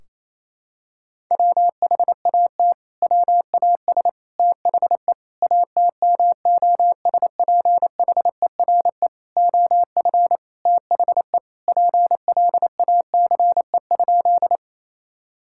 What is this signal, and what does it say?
Morse code, heard as a keyed tone, about 28 wpm, 700 Hz. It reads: WHAT WAS THE ATMOSPHERE OF THE PLACE?
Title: Sentences from Top 1600 Words 28wpm